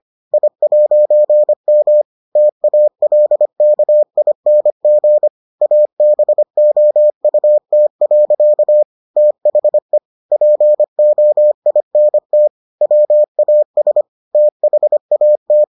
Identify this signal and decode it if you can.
Morse code, heard as a keyed tone, about 25 wpm, 600 Hz. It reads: I'M TALKING ABOUT. THE POINT WAS THAT